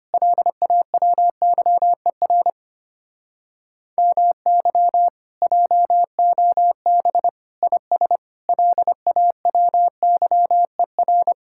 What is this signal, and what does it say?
Morse code, heard as a keyed tone, about 32 wpm, 700 Hz. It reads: LAWYER MY JOB IS LAWYER